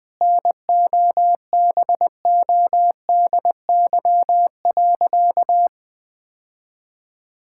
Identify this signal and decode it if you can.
Morse code, heard as a keyed tone, about 20 wpm, 700 Hz. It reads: NOBODY.
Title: Sentences from Top 1600 Words 20wpm